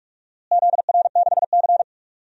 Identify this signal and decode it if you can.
Morse code, heard as a keyed tone, about 44 wpm, 700 Hz. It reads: ZR6C
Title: DXCC-100 Call Signs - 4 Characters - Repeated 3x - 45wpm